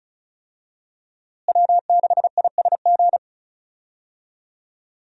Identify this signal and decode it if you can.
Morse code, heard as a keyed tone, about 34 wpm, 700 Hz. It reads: W6ISG
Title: USA Call Signs - 5 Characters - Repeated 2x - 35wpm